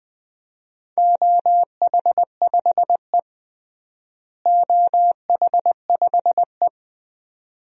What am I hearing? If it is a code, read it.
Morse code, heard as a keyed tone, about 20 wpm, 700 Hz. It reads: OH5E OH5E
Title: DXCC-100 Call Signs - 4 Characters - Repeated 2x - 20wpm